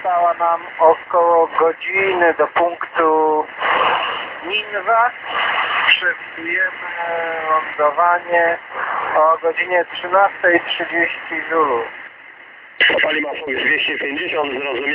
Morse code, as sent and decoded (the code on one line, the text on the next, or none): none